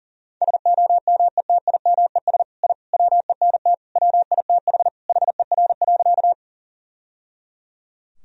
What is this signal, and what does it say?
Morse code, heard as a keyed tone, about 37 wpm, 700 Hz. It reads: SOMETIMES I WENT WITH HER.
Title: Sentences from Top 1000 Words - Review Part 3 - 40wpm